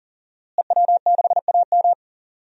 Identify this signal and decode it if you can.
Morse code, heard as a keyed tone, about 37 wpm, 700 Hz. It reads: EW6AM
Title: DXCC-100 Call Signs - 5 Characters - Repeated 3x - 40wpm